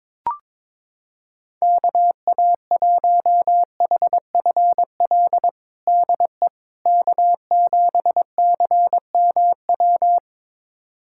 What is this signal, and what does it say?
Morse code, heard as a keyed tone, about 22 wpm, 700 Hz. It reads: KA1HFL DE K7CMW
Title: USA Callsign DE Callsign 22wpm